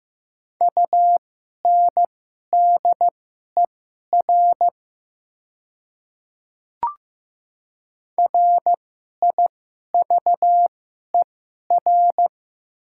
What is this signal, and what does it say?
Morse code, heard as a keyed tone, about 15 wpm, spaced as for 11 wpm, 700 Hz. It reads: UNDER RIVER
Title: 2x Character Spacing - Top 500 Words - 15wpm